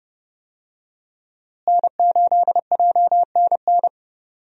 Morse code, heard as a keyed tone, about 31 wpm, 700 Hz. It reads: N8JNN
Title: USA Call Signs - 5 Characters - Repeated 2x - 30wpm